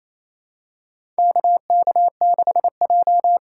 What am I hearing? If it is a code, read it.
Morse code, heard as a keyed tone, about 28 wpm, 700 Hz. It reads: KK6J
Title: USA Call Signs - 4 Characters - 6x Character Spacing - 28wpm